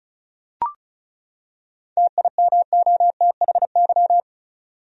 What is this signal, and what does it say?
Morse code, heard as a keyed tone, about 36 wpm, 700 Hz. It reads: TIMOTHY
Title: USA Common Names - 35wpm